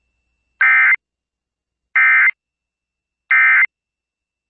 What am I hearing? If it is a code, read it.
Emergency Alert System (SAME) tones, heard as harsh buzzing bursts. It is heard from 0.6 to 3.6 s.